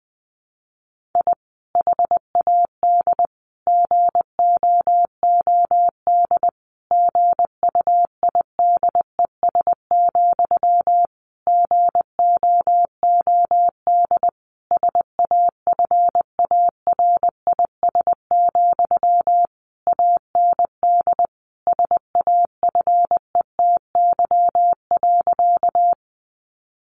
Morse code, heard as a keyed tone, about 20 wpm, 700 Hz. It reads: I HAD GOOD GUIDES, GOOD SAFARIS, AND SAFETY.
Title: A Sound Of Thunder 20wpm